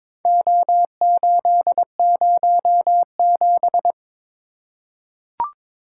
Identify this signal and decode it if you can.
Morse code, heard as a keyed tone, about 22 wpm, 700 Hz. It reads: O807
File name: practice / 22wpm / Four Letter-Number Combinations - 22wpm